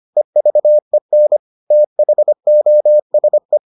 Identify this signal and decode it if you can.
Morse code, heard as a keyed tone, about 25 wpm, 600 Hz. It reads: EVEN THOSE